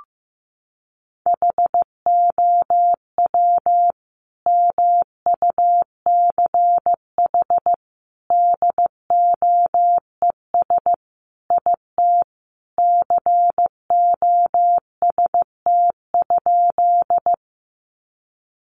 Morse code, heard as a keyed tone, about 15 wpm, 700 Hz. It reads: HOW MUCH DOES IT COST?
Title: Sentences from Top 800 Words 15wpm